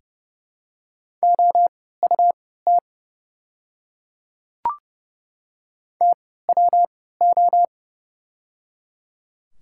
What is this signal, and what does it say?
Morse code, heard as a keyed tone, about 31 wpm, spaced as for 17 wpm, 700 Hz. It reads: OUT TWO
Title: 3x Character Spacing - Top 100 Words - 30wpm